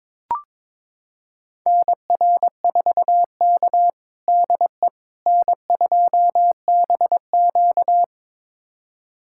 Morse code, heard as a keyed tone, about 22 wpm, 700 Hz. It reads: NR4K DE N2BQ
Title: USA Callsign DE Callsign - 4 Characters - 22wpm